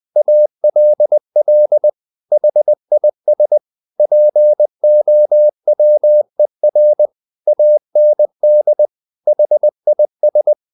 Morse code, heard as a keyed tone, about 20 wpm, 600 Hz. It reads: ALL HIS POWER AND HIS
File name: the old man and the sea part 2 20wpm